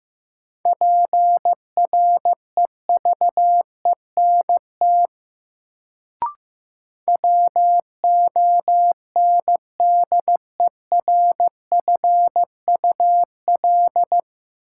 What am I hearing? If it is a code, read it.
Morse code, heard as a keyed tone, about 15 wpm, 700 Hz. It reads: PREVENT WONDERFUL
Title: Sentences from Top 1100 Words - New Word Drill - 15wpm